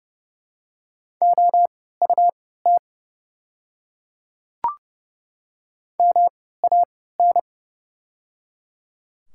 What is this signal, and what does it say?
Morse code, heard as a keyed tone, about 30 wpm, spaced as for 17 wpm, 700 Hz. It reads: OUT MAN
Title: 3x Character Spacing - Top 200 Words - 30wpm